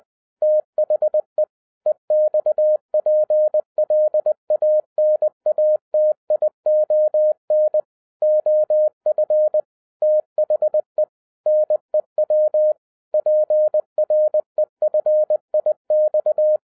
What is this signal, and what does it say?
Morse code, heard as a keyed tone, about 20 wpm, 610 Hz. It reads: THE EXPLANATION OF THE NEW PREFIX